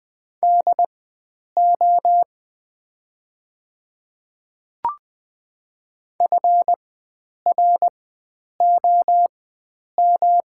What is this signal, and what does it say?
Morse code, heard as a keyed tone, about 20 wpm, spaced as for 9 wpm, 700 Hz. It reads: DO FROM
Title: 4x Character Spacing - Top 300 Words - 20wpm